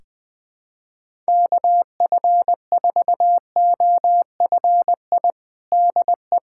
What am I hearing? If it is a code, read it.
Morse code, heard as a keyed tone, about 20 wpm, 700 Hz. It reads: KF4OFI DE
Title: USA Callsign DE Callsign - 6 Characters - 20wpm